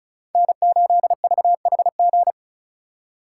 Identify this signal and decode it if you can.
Morse code, heard as a keyed tone, about 44 wpm, 700 Hz. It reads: N8VHG